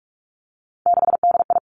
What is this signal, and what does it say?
Morse code, heard as a keyed tone, about 46 wpm, 700 Hz. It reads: 6DI
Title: Three Challenging Letter-Number Combinations 45wpm